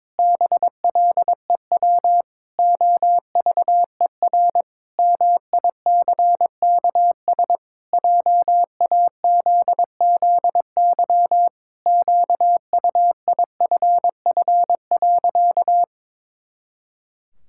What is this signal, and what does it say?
Morse code, heard as a keyed tone, about 22 wpm, 700 Hz. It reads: BLEW OVER MICKS JAZZY QUIFF.
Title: Pangram Challenge v2 - 22wpm